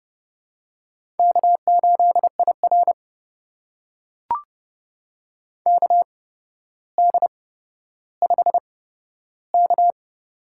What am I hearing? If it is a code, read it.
Morse code, heard as a keyed tone, about 31 wpm, spaced as for 13 wpm, 700 Hz. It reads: K8IR KD5K
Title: USA Call Signs - 4 Characters - 8x Character Spacing - 30wpm